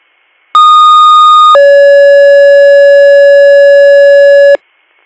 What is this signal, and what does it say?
Two-tone page, heard as a one-second tone, then a three-second tone, about 1230 then 560 Hz. It is heard from 0.5 to 4.6 s.